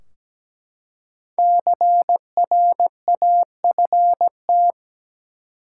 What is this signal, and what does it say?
Morse code, heard as a keyed tone, about 17 wpm, 700 Hz. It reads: CRAFT
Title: Five Letter Words - 17wpm